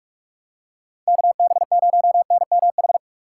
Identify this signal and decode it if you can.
Morse code, heard as a keyed tone, about 41 wpm, 700 Hz. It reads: KB0NMH